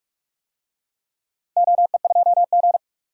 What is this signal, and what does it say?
Morse code, heard as a keyed tone, about 42 wpm, 700 Hz. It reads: OE2G
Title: DXCC-100 Call Signs - 4 Characters - 45wpm